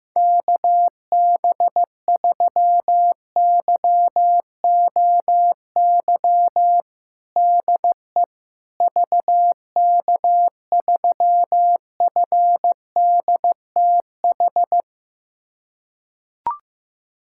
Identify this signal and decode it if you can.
Morse code, heard as a keyed tone, about 15 wpm, 700 Hz. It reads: KB3YOY DE VK3FDTH
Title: USA Callsign DE Callsign 15wpm